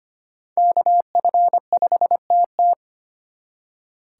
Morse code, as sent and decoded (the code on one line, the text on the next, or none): -.- ..-. ..... - -
KF5TT